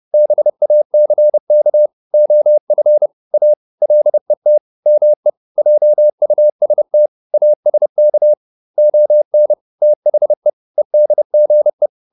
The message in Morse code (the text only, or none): BACK OF A LET ME JUST ASK ON THE EDGE